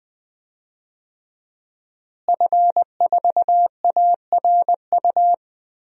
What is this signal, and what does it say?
Morse code, heard as a keyed tone, about 21 wpm, 700 Hz. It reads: F4ARU
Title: DXCC-100 Call Signs - 5 Characters - Repeated 3x - 20wpm